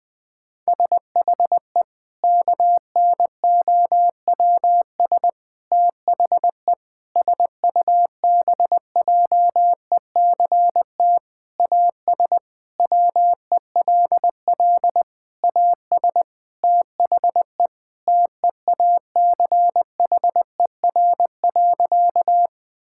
Morse code, heard as a keyed tone, about 21 wpm, 700 Hz. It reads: SHE KNOWS THE SUBJECT AS WELL AS THE TEACHER.
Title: Sentences with 4-Word Phrases - 20wpm